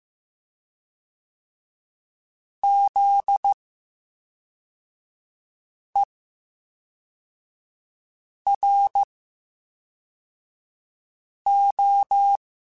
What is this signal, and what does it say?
Morse code, heard as a keyed tone, about 15 wpm, spaced as for 3 wpm, 800 Hz. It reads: ZERO
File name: Special INGs / Special INGs-mp3s / VerySlowToFaster Ing @10